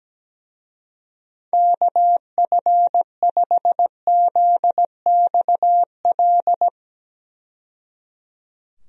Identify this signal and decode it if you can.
Morse code, heard as a keyed tone, about 17 wpm, 700 Hz. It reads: KF5ZXL
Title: USA Call Signs - 6 Characters - 17wpm